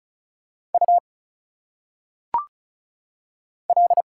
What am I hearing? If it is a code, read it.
Morse code, heard as a keyed tone, about 34 wpm, 700 Hz. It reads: U L
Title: Single Letters - Phonetic Pronunciation 35wpm